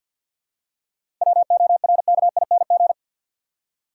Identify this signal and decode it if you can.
Morse code, heard as a keyed tone, about 48 wpm, 700 Hz. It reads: WORKING
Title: Seven Letter Words - 50wpm